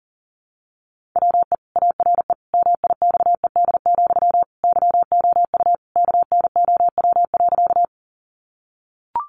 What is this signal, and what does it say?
Morse code, heard as a keyed tone, about 40 wpm, 700 Hz. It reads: WE ARE MIXED, YOU KNOW.
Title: Sentences from Top 1500 Words 40wpm